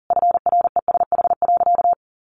Morse code, heard as a keyed tone, about 40 wpm, 700 Hz. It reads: FRESH.
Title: Sentences from Top 1500 Words 40wpm